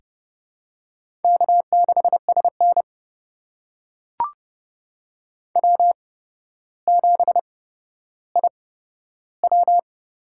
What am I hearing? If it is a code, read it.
Morse code, heard as a keyed tone, about 30 wpm, spaced as for 13 wpm, 700 Hz. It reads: K6SN W7IW
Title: USA Call Signs - 4 Characters - 8x Character Spacing - 30wpm